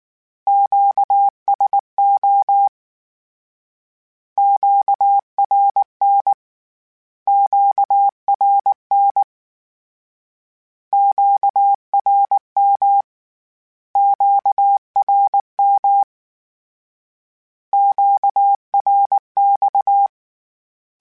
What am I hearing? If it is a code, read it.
Morse code, heard as a keyed tone, about 19 wpm, 800 Hz. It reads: QSO QRN QRN QRM QRM QRX
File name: Q Calls Twice @18